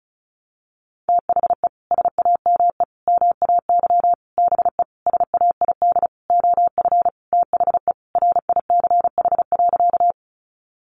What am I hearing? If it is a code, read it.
Morse code, heard as a keyed tone, about 34 wpm, 700 Hz. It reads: THE SAME MAY BE SAID OF THE RICH.
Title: Sentences from Top 1100 Words 35wpm